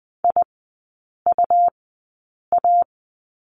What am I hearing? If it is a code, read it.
Morse code, heard as a keyed tone, about 20 wpm, spaced as for 8 wpm, 700 Hz. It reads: IUA
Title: 114 Lesson 19e - Letter w - Three Characters 20wpm